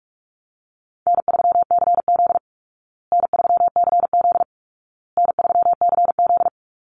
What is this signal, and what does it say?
Morse code, heard as a keyed tone, about 44 wpm, 700 Hz. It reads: N3CZ N3CZ N3CZ
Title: Call Signs Repeated 2x - 45wpm